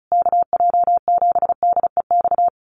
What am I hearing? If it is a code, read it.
Morse code, heard as a keyed tone, about 34 wpm, 700 Hz. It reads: KJ7DEX
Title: USA Call Signs Repeated 3x 35wpm